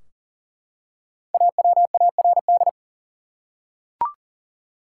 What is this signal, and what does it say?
Morse code, heard as a keyed tone, about 40 wpm, 700 Hz. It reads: AWARD
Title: Five Letter Words - 40wpm